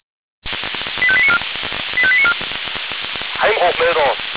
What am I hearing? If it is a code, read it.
Five-tone selective call (ZVEI), heard as quick runs of five tones, tones of about 70 ms, with speech. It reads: 85894 85894